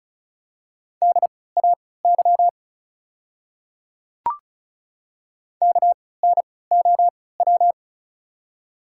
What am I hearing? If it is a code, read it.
Morse code, heard as a keyed tone, about 36 wpm, spaced as for 20 wpm, 690 Hz. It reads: DAY KNOW
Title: 3x Character Spacing - Top 100 Words - 35wpm